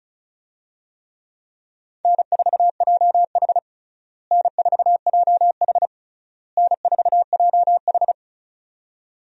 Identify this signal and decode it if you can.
Morse code, heard as a keyed tone, about 34 wpm, 700 Hz. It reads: N4JH N4JH N4JH
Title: USA Call Signs - 4 Characters - Repeated 3x - 35wpm